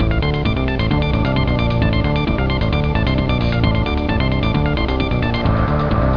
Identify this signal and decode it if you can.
There is no code.